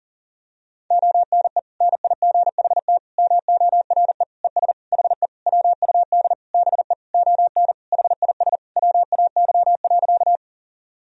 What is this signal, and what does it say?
Morse code, heard as a keyed tone, about 40 wpm, 700 Hz. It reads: ONE NIGHT MORE ES HE WUD BE ON HIS WAY.
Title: Sentences from Top 300 Words and Abbreviations 40wpm